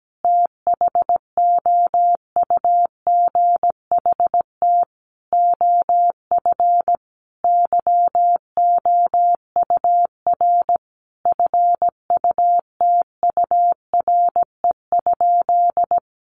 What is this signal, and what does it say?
Morse code, heard as a keyed tone, about 17 wpm, 700 Hz. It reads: THOUGHT OF YOUR FUTURE?
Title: Sentences from Top 700 Words 17wpm